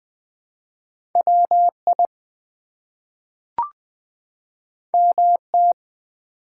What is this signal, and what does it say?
Morse code, heard as a keyed tone, about 20 wpm, 700 Hz. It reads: WI MT
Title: ARRL Sections v2 - 20wpm